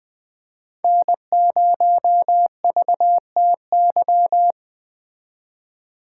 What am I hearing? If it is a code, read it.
Morse code, heard as a keyed tone, about 21 wpm, 700 Hz. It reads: N0VTY